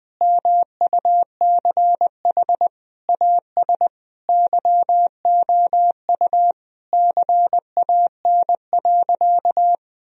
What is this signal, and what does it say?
Morse code, heard as a keyed tone, about 20 wpm, 700 Hz. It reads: MUCH AS YOU CAN.
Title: Sentences from Top 1000 Words - Review Part 1 - 20wpm